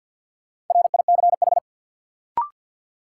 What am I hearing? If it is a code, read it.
Morse code, heard as a keyed tone, about 48 wpm, 700 Hz. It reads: RICH
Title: Sentences from Top 1100 Words - New Word Drill - 50wpm